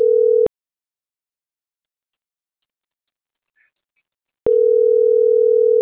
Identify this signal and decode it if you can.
Ringback tone (North America), 440 + 480 Hz.